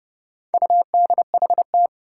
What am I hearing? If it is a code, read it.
Morse code, heard as a keyed tone, about 30 wpm, 700 Hz. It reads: UDHT
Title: Four Challenging Letter-Number Combinations 30wpm